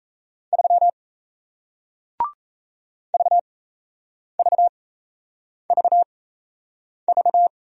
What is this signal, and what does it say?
Morse code, heard as a keyed tone, about 37 wpm, spaced as for 8 wpm, 700 Hz. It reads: 3 VVVV